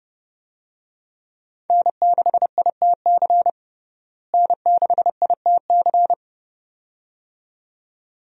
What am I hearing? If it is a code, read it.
Morse code, heard as a keyed tone, about 32 wpm, 700 Hz. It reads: N6ITC N6ITC
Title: USA Call Signs - 5 Characters - Repeated 2x - 30wpm